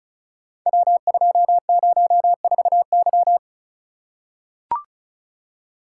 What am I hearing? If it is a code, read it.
Morse code, heard as a keyed tone, about 36 wpm, 700 Hz. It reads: W204Y
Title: Code Groups - 5 Letter:Numbers - 35wpm